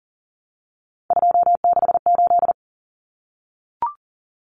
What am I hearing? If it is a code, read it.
Morse code, heard as a keyed tone, about 40 wpm, 700 Hz. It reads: 268
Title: Common Cut Numbers - 3-Digit 40wpm